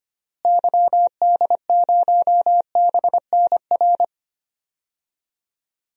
Morse code, heard as a keyed tone, about 25 wpm, 700 Hz. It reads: YD0BNR
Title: DXCC-100 Call Signs - 6 Characters - Repeated 3x - 25wpm